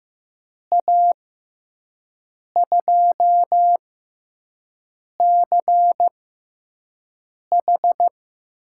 Morse code, heard as a keyed tone, about 15 wpm, spaced as for 5 wpm, 700 Hz. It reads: A2CH
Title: USA Call Signs - 6 Characters - 6x Character Spacing - 15wpm